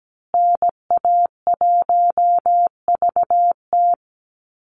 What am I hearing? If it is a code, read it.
Morse code, heard as a keyed tone, about 17 wpm, 700 Hz. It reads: NA1VT
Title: Call Signs Repeated 3x - 17wpm